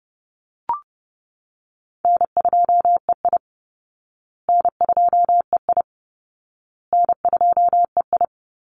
Morse code, heard as a keyed tone, about 30 wpm, 700 Hz. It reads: N2EI N2EI N2EI
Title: Easy Call Signs Repeated 3x - 30wpm